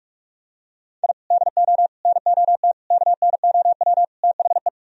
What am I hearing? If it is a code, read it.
Morse code, heard as a keyed tone, about 48 wpm, 700 Hz. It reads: I DO NOT KNOW THE